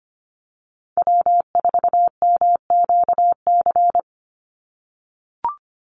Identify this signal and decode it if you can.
Morse code, heard as a keyed tone, about 25 wpm, 700 Hz. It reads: W4MQC
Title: Call Signs - 25wpm